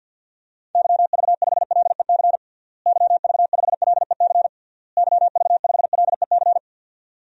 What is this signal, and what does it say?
Morse code, heard as a keyed tone, about 52 wpm, 700 Hz. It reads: YV5LEC YV5LEC YV5LEC
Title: DXCC-100 Call Signs - 6 Characters - Repeated 3x - 50wpm